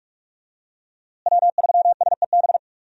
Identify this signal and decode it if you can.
Morse code, heard as a keyed tone, about 44 wpm, 700 Hz. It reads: W3SEB
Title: USA Call Signs - 5 Characters - Repeated 2x - 45wpm